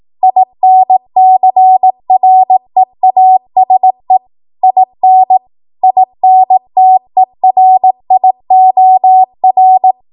Morse code, heard as a keyed tone, about 18 wpm, 750 Hz. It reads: INCREASE IN INTERIOR